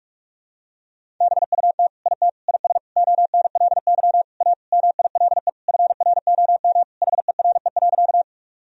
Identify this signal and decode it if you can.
Morse code, heard as a keyed tone, about 42 wpm, 700 Hz. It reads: BUT IT IS ONLY A MILE FROM HERE.
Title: Sentences from Top 1000 Words - Review Part 1 - 45wpm